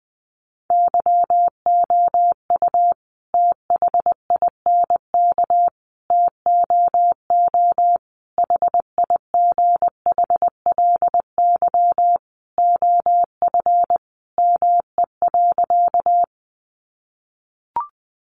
Morse code, heard as a keyed tone, about 20 wpm, 700 Hz. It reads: YOU THINK TOO HIGHLY OF ME.
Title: Sentences from Top 1100 Words 20wpm